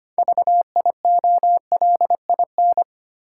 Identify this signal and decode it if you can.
Morse code, heard as a keyed tone, about 25 wpm, 690 Hz. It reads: VIOLIN
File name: practice / 25wpm / Easy Words with Hints - 25wpm